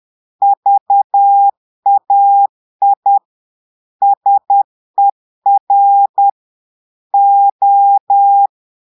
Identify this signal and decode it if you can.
Morse code, heard as a keyed tone, about 10 wpm, 800 Hz. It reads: VAI SER O
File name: portugais texte en clair 2 vitesse 10 mots minutes